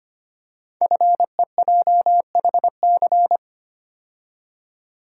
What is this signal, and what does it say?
Morse code, heard as a keyed tone, about 25 wpm, 700 Hz. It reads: FEJHC
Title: Code Groups - 5 Letters - 25wpm